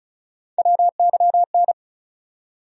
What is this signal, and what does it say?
Morse code, heard as a keyed tone, about 36 wpm, 700 Hz. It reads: WYN
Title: USA Call Sign Postfixes - 3 Letters 35wpm